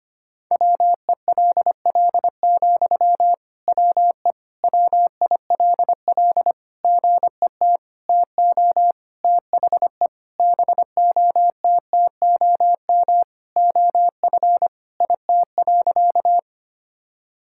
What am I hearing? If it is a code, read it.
Morse code, heard as a keyed tone, about 25 wpm, 700 Hz. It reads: WELL, WE WILL GET TO THE BOTTOM OF IT.
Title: Sentences from Top 1000 Words - Review Part 6 - 25wpm